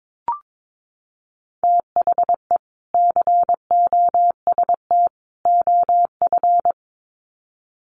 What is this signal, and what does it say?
Morse code, heard as a keyed tone, about 22 wpm, 700 Hz. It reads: THE COST OF
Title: Sets of 3 Words 22wpm